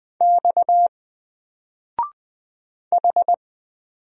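Morse code, heard as a keyed tone, about 20 wpm, 700 Hz. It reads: X H
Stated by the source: Single letter and number Morse drill at 20 wpm, from N4LCD.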